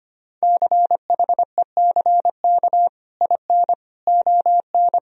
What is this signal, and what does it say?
Morse code, heard as a keyed tone, about 25 wpm, 700 Hz. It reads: CHECK IN ON
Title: Sentences from Top 1000 Words - Review Part 3 - 25wpm